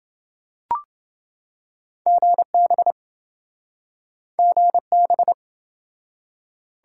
Morse code, heard as a keyed tone, about 28 wpm, 700 Hz. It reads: GB GB